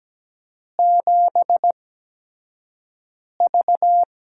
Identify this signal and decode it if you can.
Morse code, heard as a keyed tone, about 17 wpm, spaced as for 5 wpm, 700 Hz. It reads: 7V
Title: USA Call Signs - 8x Character Spacing - 17wpm